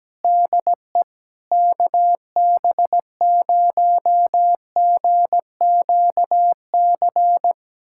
Morse code, heard as a keyed tone, about 17 wpm, 700 Hz. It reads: DE KB0GQC
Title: USA Callsign DE Callsign - 6 Characters - 17wpm